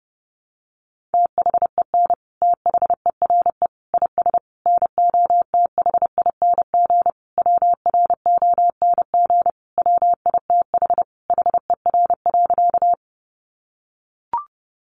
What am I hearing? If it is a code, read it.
Morse code, heard as a keyed tone, about 30 wpm, 700 Hz. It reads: THEN THERE IS NOTHING WRONG WITH HER.
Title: Sentences from Top 900 Words 30wpm